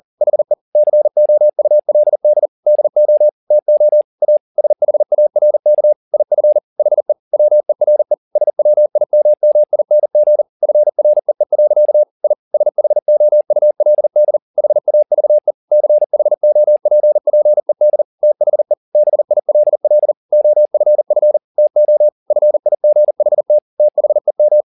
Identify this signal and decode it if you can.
Morse code, heard as a keyed tone, about 40 wpm, 600 Hz. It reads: HE COULD DO TO A SHARK IF HE WERE SWIMMING FREE. I SHOULD HAVE CHOPPED THE BILL OFF TO FIGHT THEM
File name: the old man and the sea 2 40wpm